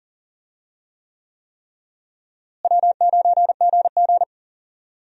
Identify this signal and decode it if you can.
Morse code, heard as a keyed tone, about 40 wpm, 700 Hz. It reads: W9GG
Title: USA Call Signs - 4 Characters - Repeated 3x - 40wpm